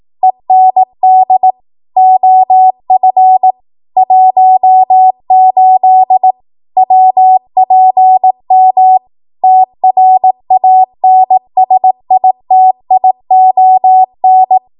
Morse code, heard as a keyed tone, about 18 wpm, 750 Hz. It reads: END OF 18 WPM TRANSITION
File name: Morse / 18WPM